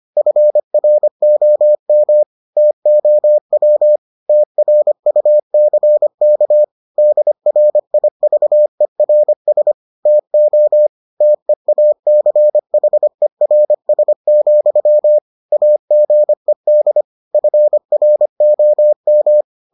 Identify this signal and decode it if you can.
Morse code, heard as a keyed tone, about 25 wpm, 600 Hz. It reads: FROM TOW TRUCK DRIVERS TO TEACHERS, AGED FROM